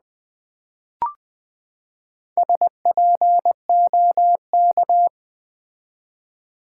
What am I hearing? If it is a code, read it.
Morse code, heard as a keyed tone, about 20 wpm, 700 Hz. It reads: SPOK
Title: Build A Word - 4 Letters At A Time - 12-Letter Words - 20wpm